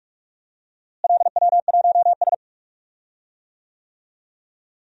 Morse code, heard as a keyed tone, about 46 wpm, 700 Hz. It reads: LW1S
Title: DXCC-100 Call Signs - 4 Characters - Repeated 2x - 45wpm